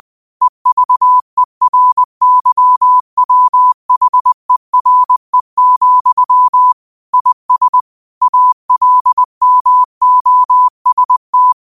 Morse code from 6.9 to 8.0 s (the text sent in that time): IS